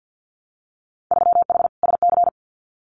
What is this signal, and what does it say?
Morse code, heard as a keyed tone, about 48 wpm, 700 Hz. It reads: 3H SC
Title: Winter Field Day Exchanges 50wpm